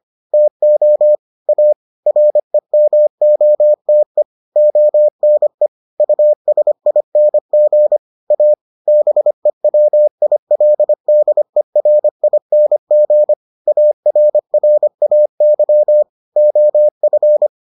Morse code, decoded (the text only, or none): TO A REMOTE ONE USING A BEWILDERING ARRAY OF